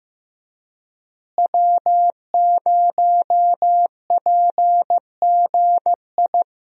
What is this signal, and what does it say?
Morse code, heard as a keyed tone, about 15 wpm, 700 Hz. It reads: W0PGI
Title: USA Call Signs - 5 Characters - Repeated 3x - 15wpm